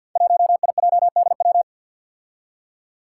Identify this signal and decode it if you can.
Morse code, heard as a keyed tone, about 51 wpm, 700 Hz. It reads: 1IJDW
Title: Five Challenging Letter-Number Combinations 50wpm